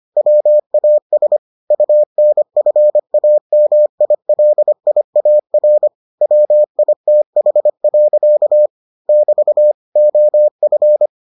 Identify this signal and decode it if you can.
Morse code, heard as a keyed tone, about 25 wpm, 600 Hz. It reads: WAS UNFAMILIAR WITH. <BT> OF